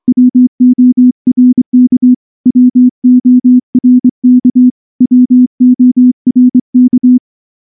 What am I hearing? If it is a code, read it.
Morse code, heard as a keyed tone, about 24 wpm, 250 Hz. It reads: WORK WORK WORK
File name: morse-training-25WPM